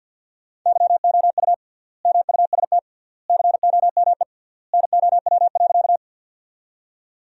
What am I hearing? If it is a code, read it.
Morse code, heard as a keyed tone, about 50 wpm, 700 Hz. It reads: YOU MUST COME NOW.
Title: 3x Word-Spacing - Sentences from Top 200 Words - 50wpm